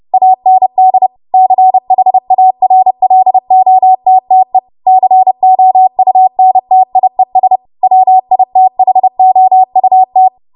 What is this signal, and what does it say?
Morse code, heard as a keyed tone, about 30 wpm, 750 Hz. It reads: AND CHARLOTTE COUNTIES WITHOUT